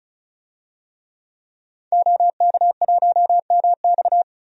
Morse code, heard as a keyed tone, about 36 wpm, 700 Hz. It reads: OK1MX
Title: DXCC-100 Call Signs - 5 Characters - 35wpm